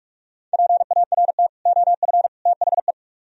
Morse code, heard as a keyed tone, about 55 wpm, 700 Hz. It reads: PART OF THE
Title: Speed-Racing - Sets of 3 Words 45wpm